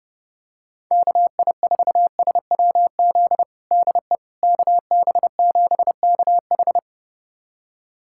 Morse code, heard as a keyed tone, about 29 wpm, 700 Hz. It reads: KI4SWZ DE KB7KH
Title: USA Callsign DE Callsign 30wpm